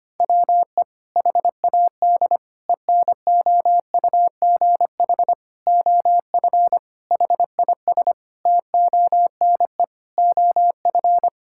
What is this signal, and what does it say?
Morse code, heard as a keyed tone, about 25 wpm, 700 Hz. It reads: WE HAD ENOUGH OF HIS TONE OF
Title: Sentences from Top 1600 Words 25wpm